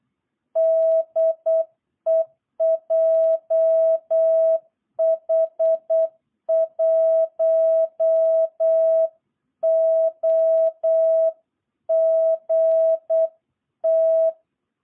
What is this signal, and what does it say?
Morse code, heard as a keyed tone, about 8 wpm, 660 Hz. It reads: DE JH1OGT
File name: jh8cbh / 2022cw